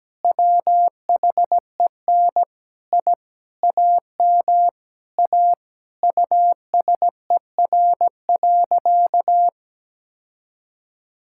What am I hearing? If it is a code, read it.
Morse code, heard as a keyed tone, about 17 wpm, 700 Hz. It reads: WHEN I AM A USER.
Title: Sentences from Top 1000 Words - Review Part 5 - 17wpm